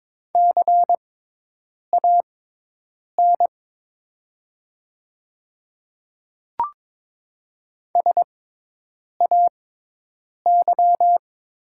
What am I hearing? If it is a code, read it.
Morse code, heard as a keyed tone, about 22 wpm, spaced as for 8 wpm, 700 Hz. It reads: CAN SAY
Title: 6x Character Spacing - Top 300 Words - 22wpm